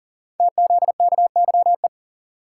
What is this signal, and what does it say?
Morse code, heard as a keyed tone, about 40 wpm, 700 Hz. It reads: TZKYE
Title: Code Groups - 5 Letters - 40wpm